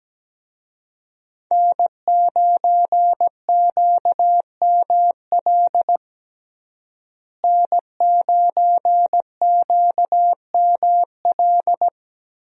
Morse code, heard as a keyed tone, about 17 wpm, 700 Hz. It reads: N9QML N9QML
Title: USA Call Signs - 5 Characters - Repeated 2x - 17wpm